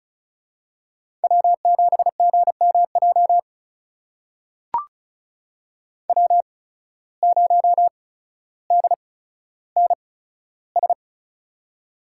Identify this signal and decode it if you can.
Morse code, heard as a keyed tone, about 36 wpm, spaced as for 15 wpm, 700 Hz. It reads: W7GMJ W0DNS
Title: USA Call Signs - 8x Character Spacing - 35wpm